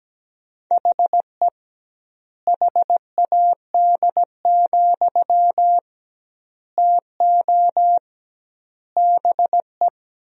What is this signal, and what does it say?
Morse code, heard as a keyed tone, about 17 wpm, 700 Hz. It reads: HE HAD, TO BE
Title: 2x Word-Spacing - Sentences from Top 500 Words - 17wpm